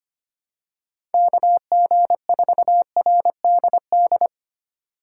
Morse code, heard as a keyed tone, about 25 wpm, 700 Hz. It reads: KG4RDD
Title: USA Call Signs - 6 Characters - 3x Character Spacing - 25wpm